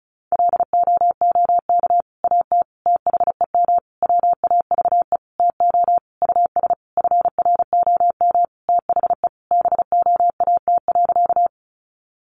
Morse code, heard as a keyed tone, about 34 wpm, 700 Hz. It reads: LOOK AT THEM WAVE TO US FROM THE BOAT.
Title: Sentences from Top 1400 Words 35wpm